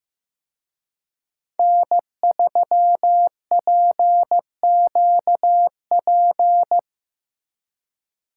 Morse code, heard as a keyed tone, about 18 wpm, 700 Hz. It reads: N3PQP
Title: Speed Racing - USA Call Signs 15wpm